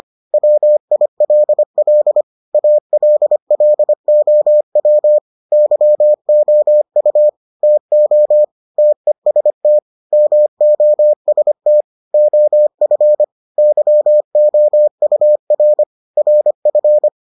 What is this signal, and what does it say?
Morse code, heard as a keyed tone, about 25 wpm, 600 Hz. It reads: WILL ALLOW YOU TO TEST MOST OF YOUR RF